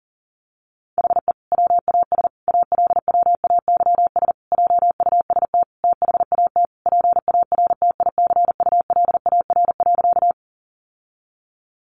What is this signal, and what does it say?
Morse code, heard as a keyed tone, about 40 wpm, 700 Hz. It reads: HE WAS ALWAYS JUST THAT PARTICULAR.
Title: Sentences from Top 600 Words 40wpm